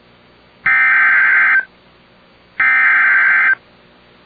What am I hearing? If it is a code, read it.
Emergency Alert System (SAME) tones (header bursts), heard as harsh buzzing bursts. It is heard from 0.7 to 3.5 s.